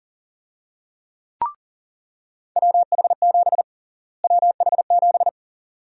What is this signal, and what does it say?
Morse code, heard as a keyed tone, about 40 wpm, 700 Hz. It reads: WH7 WH7
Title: USA Call Sign Prefixes - Repeated 2x - 40wpm